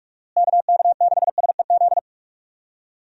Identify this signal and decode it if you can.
Morse code, heard as a keyed tone, about 46 wpm, 700 Hz. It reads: KK6SEZ
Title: USA Call Signs - 6 Characters - Repeated 2x - 45wpm